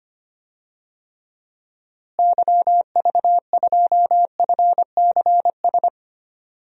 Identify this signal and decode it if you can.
Morse code, heard as a keyed tone, about 25 wpm, 700 Hz. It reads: YV2FCS
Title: DXCC-100 Call Signs - 6 Characters - Repeated 2x - 25wpm